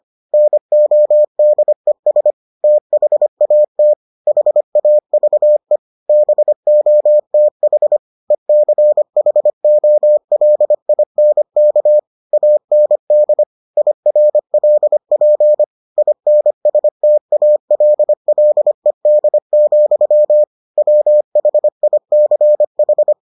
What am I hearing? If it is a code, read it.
Morse code, heard as a keyed tone, about 25 wpm, 600 Hz. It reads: NODES THAT HAVE BOTH ECHOLINK AND IRLP INSTALLED, WHICH